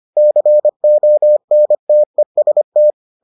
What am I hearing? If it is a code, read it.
Morse code, heard as a keyed tone, about 25 wpm, 600 Hz. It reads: CONTEST